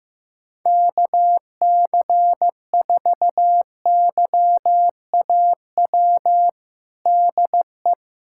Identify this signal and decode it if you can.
Morse code, heard as a keyed tone, about 15 wpm, 700 Hz. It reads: KC4YAW DE
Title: USA Callsign DE Callsign 15wpm